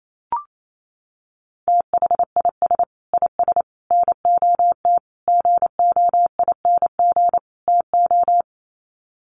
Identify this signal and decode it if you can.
Morse code, heard as a keyed tone, about 28 wpm, 700 Hz. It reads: THIS IS NOT GOING TO
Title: Sets of 5 Words Encore2 28wpm